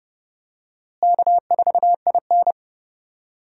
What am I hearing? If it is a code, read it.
Morse code, heard as a keyed tone, about 32 wpm, 700 Hz. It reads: K4IN
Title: USA Call Signs - 4 Characters - Repeated 3x - 30wpm